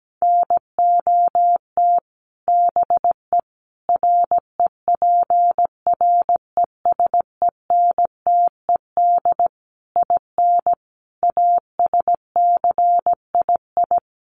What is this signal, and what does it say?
Morse code, heard as a keyed tone, about 17 wpm, 700 Hz. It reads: NOT BE REPRESENTED IN ASCII
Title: Sets of 5 Words 17wpm